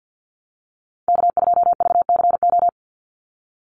Morse code, heard as a keyed tone, about 50 wpm, 700 Hz. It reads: K2VCO
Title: Call Signs - 50wpm